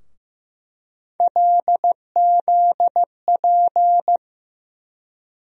Morse code, heard as a keyed tone, about 15 wpm, 700 Hz. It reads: LZP
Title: USA Call Sign Postfixes - 3 Letters 15wpm